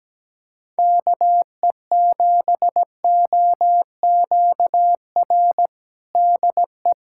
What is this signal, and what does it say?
Morse code, heard as a keyed tone, about 17 wpm, 700 Hz. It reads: KE7OQR DE
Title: USA Callsign DE Callsign - 6 Characters - 17wpm